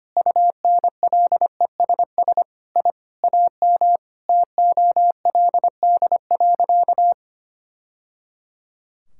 Morse code, heard as a keyed tone, about 25 wpm, 700 Hz. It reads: UNLESS I AM TOLD.
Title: Sentences from Top 1000 Words - Review Part 6 - 25wpm